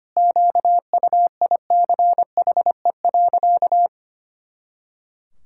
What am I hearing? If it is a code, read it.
Morse code, heard as a keyed tone, about 25 wpm, 700 Hz. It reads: QUICHE.
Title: Pangram Challenge v2 - 25wpm